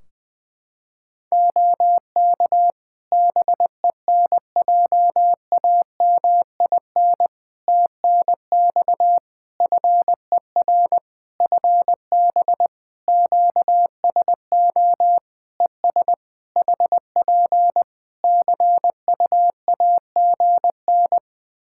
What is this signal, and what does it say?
Morse code, heard as a keyed tone, about 20 wpm, 700 Hz. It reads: OK BENJAMIN TNX FER FB QSO ES HP CUAGN
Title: K1USN SST QSO Practice 20wpm